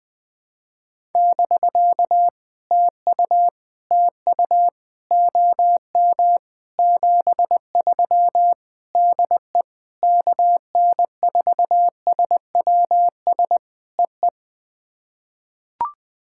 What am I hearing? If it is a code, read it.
Morse code, heard as a keyed tone, about 20 wpm, 700 Hz. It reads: <BK> TU TU OM 73 DE KN4SWS EE
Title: POTA QSO Practice - 20wpm